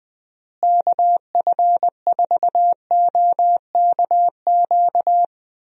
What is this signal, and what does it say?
Morse code, heard as a keyed tone, about 20 wpm, 700 Hz. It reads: KF4OKQ
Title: USA Call Signs - 6 Characters - Repeated 3x - 20wpm